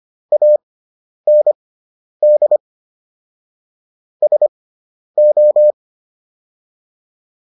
Morse code, heard as a keyed tone, about 25 wpm, spaced as for 10 wpm, 600 Hz. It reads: AND SO